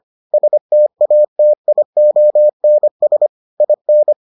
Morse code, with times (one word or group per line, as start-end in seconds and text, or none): 0.3-3.3 STATIONS
3.6-4.1 IN